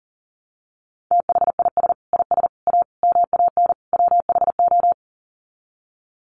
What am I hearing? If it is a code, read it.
Morse code, heard as a keyed tone, about 40 wpm, 700 Hz. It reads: THIS IS A MAN WHO
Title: Sets of 5 Words Encore2 40wpm